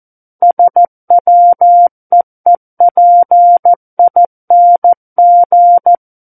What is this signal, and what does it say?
Morse code, heard as a keyed tone, about 14 wpm, 700 Hz. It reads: SWEEPING
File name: Book One / 05-Chapter 5 - 14 WPM